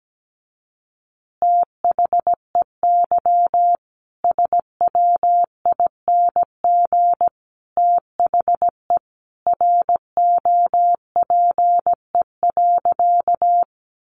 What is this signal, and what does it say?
Morse code, heard as a keyed tone, about 17 wpm, 700 Hz. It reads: THEY SWING THE ROPE.
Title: Easy Phrases 17wpm